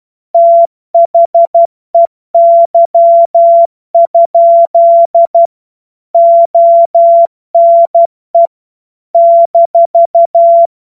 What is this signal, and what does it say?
Morse code, heard as a keyed tone, about 12 wpm, 680 Hz. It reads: THEY? ONE -
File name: Morsekurs / Witze